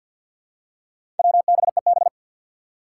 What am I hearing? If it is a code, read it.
Morse code, heard as a keyed tone, about 50 wpm, 700 Hz. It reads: WBEB
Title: Four Challenging Letter-Number Combinations 50wpm